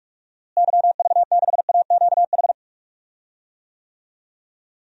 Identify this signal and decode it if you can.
Morse code, heard as a keyed tone, about 47 wpm, 700 Hz. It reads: YV6AQH
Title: DXCC-100 Call Signs - 6 Characters - Repeated 2x - 45wpm